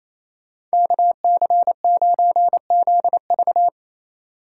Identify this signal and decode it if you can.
Morse code, heard as a keyed tone, about 33 wpm, 700 Hz. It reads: KC9ZV
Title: Speed Racing - USA 2x2 Call Signs 28wpm